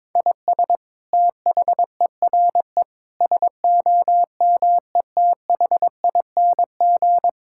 Morse code, heard as a keyed tone, about 22 wpm, 700 Hz. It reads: IS THERE SOMETHING
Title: Sentences from Top 1000 Words - Review Part 6 - 22wpm